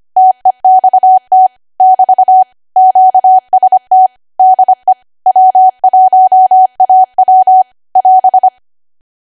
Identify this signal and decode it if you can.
Morse code, heard as a keyed tone, about 25 wpm, 750 Hz. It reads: TEXT <BT> QST DE W1AW <AS>
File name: Archive / 25 WPM / 25WPM